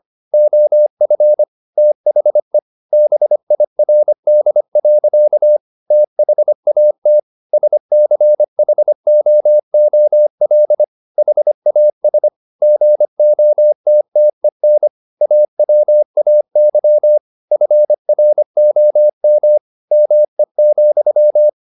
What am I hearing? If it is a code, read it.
Morse code, heard as a keyed tone, about 25 wpm, 600 Hz. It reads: OF THE BIRD. THAT SCHOOL HAS GOTTEN AWAY FROM ME,
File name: the old man and the sea part 1 25wpm